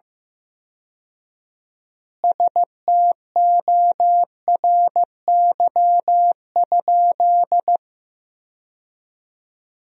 Morse code, heard as a keyed tone, about 15 wpm, 700 Hz. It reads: STORY?
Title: 4x Word-Spacing - Sentences from Top 400 Words - 15wpm